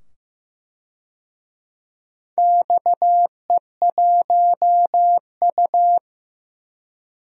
Morse code, heard as a keyed tone, about 15 wpm, 700 Hz. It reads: XE1U
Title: DXCC-100 Call Signs - 4 Characters - 15wpm